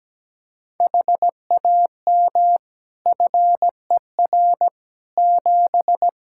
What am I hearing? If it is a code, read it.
Morse code, heard as a keyed tone, about 17 wpm, 700 Hz. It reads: HAM FER 7
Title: Practice QSO - Exchange 3 - More Info - 17wpm